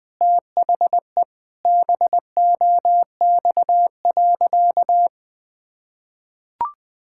Morse code, heard as a keyed tone, about 20 wpm, 700 Hz. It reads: THE BOX.
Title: Sentences with 3-Word Phrases - 20wpm